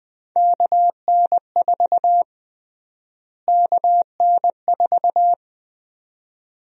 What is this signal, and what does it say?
Morse code, heard as a keyed tone, about 20 wpm, 700 Hz. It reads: KN4 KN4
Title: USA Call Sign Prefixes - Repeated 2x - 20wpm